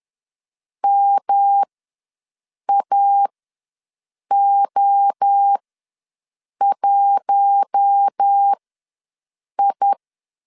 Morse code, heard as a keyed tone, about 10 wpm, spaced as for 6 wpm, 800 Hz. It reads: MAO1I
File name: LEZIONI / 5 LBVC / 5 LBV ws 10wpm 3645